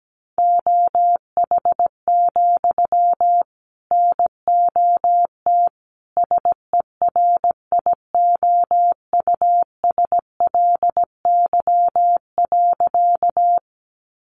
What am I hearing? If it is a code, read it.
Morse code, heard as a keyed tone, about 17 wpm, 700 Hz. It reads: OH, NOT SERIOUSLY.
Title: Sentences from Top 1200 Words 17wpm